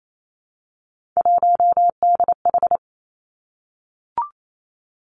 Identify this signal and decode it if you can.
Morse code, heard as a keyed tone, about 28 wpm, 700 Hz. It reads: 1DH
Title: Three Challenging Letter-Number Combinations 28wpm